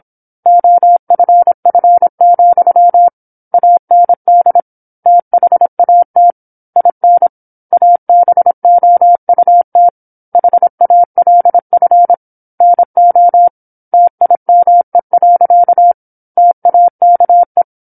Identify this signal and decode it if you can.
Morse code, heard as a keyed tone, about 27 wpm, 700 Hz. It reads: OFF, AND THAT IN ABOUT HALF NO TIME. TAKE
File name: 08-Chapter-08 26 WPM